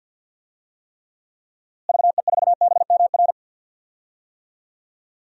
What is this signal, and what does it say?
Morse code, heard as a keyed tone, about 53 wpm, 700 Hz. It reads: VE4BMR
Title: DXCC-100 Call Signs - 6 Characters - Repeated 3x - 50wpm